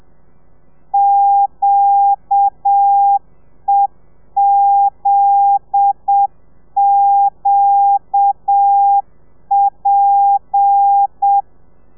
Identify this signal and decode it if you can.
Morse code, heard as a keyed tone, about 7 wpm, 800 Hz. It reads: QEZQP